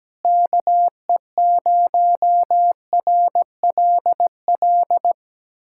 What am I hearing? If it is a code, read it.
Morse code, heard as a keyed tone, about 17 wpm, 700 Hz. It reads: KE0RLL
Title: USA Callsign DE Callsign - 6 Characters - 17wpm